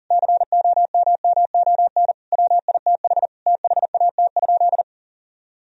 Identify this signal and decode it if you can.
Morse code, heard as a keyed tone, about 37 wpm, 700 Hz. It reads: COMMON WITH THAT?
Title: Sentences from Top 1000 Words - Review Part 4 - 40wpm